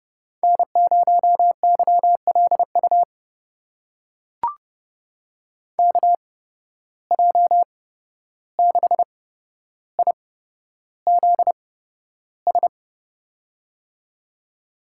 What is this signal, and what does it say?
Morse code, heard as a keyed tone, about 30 wpm, spaced as for 12 wpm, 700 Hz. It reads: N0YLU KJ6IZS
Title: USA Call Signs - 8x Character Spacing - 30wpm